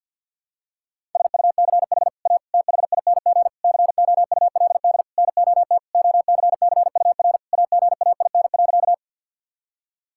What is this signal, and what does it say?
Morse code, heard as a keyed tone, about 52 wpm, 700 Hz. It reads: SUCH A THING COULD NOT OCCUR AGAIN.
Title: Sentences from Top 1000 Words - Review Part 6 - 50wpm